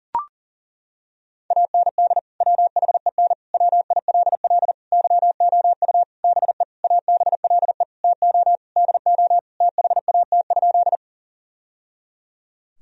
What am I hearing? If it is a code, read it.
Morse code, heard as a keyed tone, about 40 wpm, 700 Hz. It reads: AND WHEN WILL YOU BE ABLE TO DO THAT?
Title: Sentences from Top 1000 Words - Review Part 2 - 40wpm